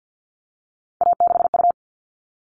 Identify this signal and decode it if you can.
Morse code, heard as a keyed tone, about 50 wpm, 700 Hz. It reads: A6U
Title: All Cut Numbers - 3-Digit 50wpm